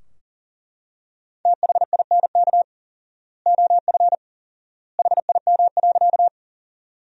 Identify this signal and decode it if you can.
Morse code, heard as a keyed tone, about 40 wpm, 700 Hz. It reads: THINK OF HIM.
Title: 4x Word-Spacing - Sentences from Top 200 Words - 40wpm